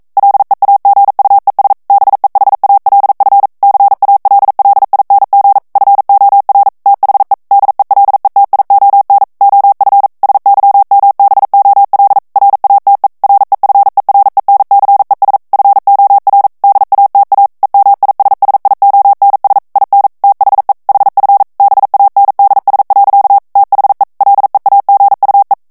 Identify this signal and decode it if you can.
Morse code, heard as a keyed tone, about 42 wpm, 800 Hz. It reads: LEAGUES BEHALF CALLING FOR THE DELETION OF SYMBOL RATE REFERENCES FOR DATA EMISSIONS IN THE HF BANDS. THE LEAGUE
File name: 40 mots minute